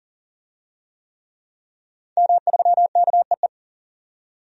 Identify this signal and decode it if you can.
Morse code, heard as a keyed tone, about 40 wpm, 700 Hz. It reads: M3KEE
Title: DXCC-100 Call Signs - 5 Characters - 40wpm